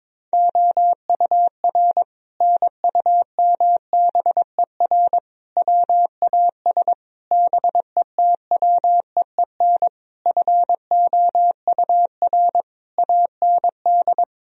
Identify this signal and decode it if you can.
Morse code, heard as a keyed tone, about 22 wpm, 700 Hz. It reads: OUR NUMBER WAS BETWEEN FOUR AND
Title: Sentences from Top 1000 Words - Review Part 2 - 22wpm